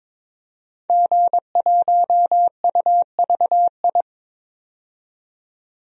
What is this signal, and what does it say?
Morse code, heard as a keyed tone, about 23 wpm, 700 Hz. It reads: G1UVI
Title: DXCC-100 Call Signs - 5 Characters - 22wpm